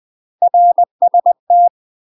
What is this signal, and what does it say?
Morse code, heard as a keyed tone, about 20 wpm, 700 Hz. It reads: RST
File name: UK MASTER QSOs TRAINING / 20WPM / UK MASTER QSOs 0034